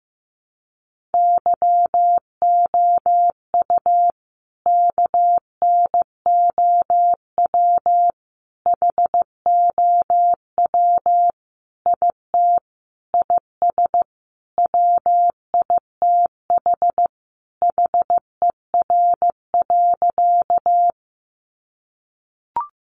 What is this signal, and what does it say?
Morse code, heard as a keyed tone, about 14 wpm, 700 Hz. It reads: YOU KNOW HOW IT IS WITH HER.
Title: Sentences from Top 100 Words 15wpm